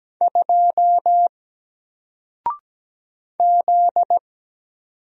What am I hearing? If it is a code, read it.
Morse code, heard as a keyed tone, about 17 wpm, 700 Hz. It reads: 2 Z
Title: Single Letter-Number Alt 17wpm